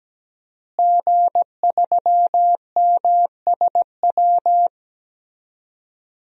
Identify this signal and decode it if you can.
Morse code, heard as a keyed tone, about 17 wpm, 700 Hz. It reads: G3MSW
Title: DXCC-100 Call Signs - 5 Characters - Repeated 3x - 17wpm